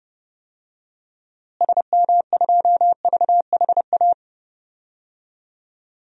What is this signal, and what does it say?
Morse code, heard as a keyed tone, about 30 wpm, 700 Hz. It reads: SM2VHA
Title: DXCC-100 Call Signs - 6 Characters - Repeated 2x - 30wpm